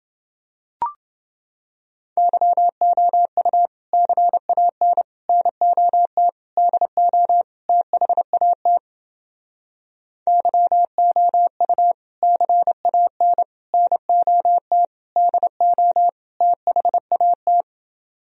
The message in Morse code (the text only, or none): YOU CAN NOT DO THAT YOU CAN NOT DO THAT